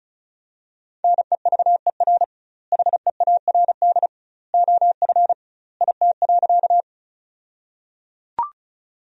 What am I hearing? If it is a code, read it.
Morse code, heard as a keyed tone, about 36 wpm, 700 Hz. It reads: NEVER HEARD OF IT.
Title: 2x Word-Spacing - Sentences from Top 400 Words - 35wpm